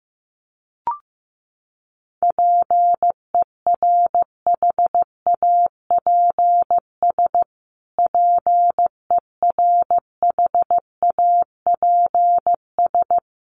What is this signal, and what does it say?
Morse code, heard as a keyed tone, about 15 wpm, 700 Hz. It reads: PERHAPS PERHAPS
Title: Top 500 Words Repeated 15wpm